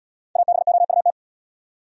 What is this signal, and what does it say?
Morse code, heard as a keyed tone, about 70 wpm, 700 Hz. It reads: ABOUT